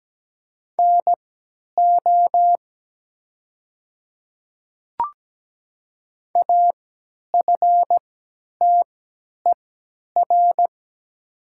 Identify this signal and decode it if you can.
Morse code, heard as a keyed tone, about 17 wpm, spaced as for 10 wpm, 700 Hz. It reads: NO AFTER
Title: 3x Character Spacing - Top 200 Words - 17wpm